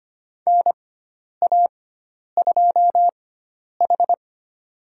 Morse code, heard as a keyed tone, about 25 wpm, spaced as for 10 wpm, 700 Hz. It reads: NA2H
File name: Lesson 16g - Letter c - Call Signs 25wpm